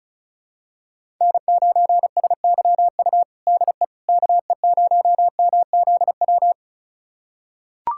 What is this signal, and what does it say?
Morse code, heard as a keyed tone, about 36 wpm, 700 Hz. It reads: N9SYU DE KE0MZW
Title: USA Callsign DE Callsign 35wpm